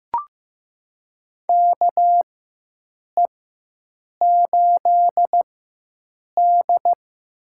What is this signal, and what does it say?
Morse code, heard as a keyed tone, about 15 wpm, spaced as for 7 wpm, 700 Hz. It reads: KE8D
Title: USA Call Signs - 4x Character Spacing - 15wpm